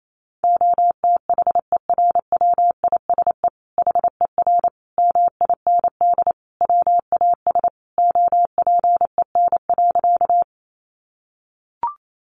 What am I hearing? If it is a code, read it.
Morse code, heard as a keyed tone, about 28 wpm, 700 Hz. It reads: OTHERWISE HER MIND WAS OPEN.
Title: Sentences from Top 1200 Words 28wpm